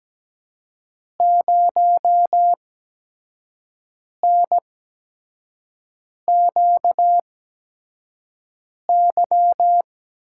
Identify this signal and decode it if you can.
Morse code, heard as a keyed tone, about 17 wpm, spaced as for 5 wpm, 700 Hz. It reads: 0NQY
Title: USA Call Signs - 5 Characters - 8x Character Spacing - 17wpm